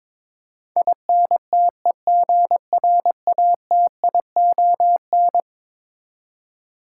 Morse code, heard as a keyed tone, about 22 wpm, 700 Hz. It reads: INTEGRATION
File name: practice / 22wpm / Eleven Letter Words - 22wpm